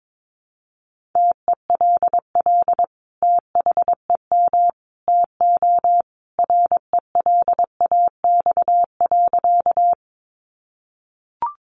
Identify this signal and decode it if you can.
Morse code, heard as a keyed tone, about 22 wpm, 700 Hz. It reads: TELL THEM TO RELAX.
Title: Sentences from Top 1400 Words 22wpm